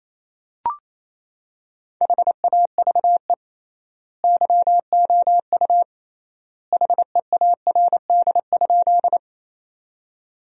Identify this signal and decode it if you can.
Morse code, heard as a keyed tone, about 28 wpm, 700 Hz. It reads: HAVE YOU HEARD?
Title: 3x Word-Spacing - Sentences from Top 400 Words - 28wpm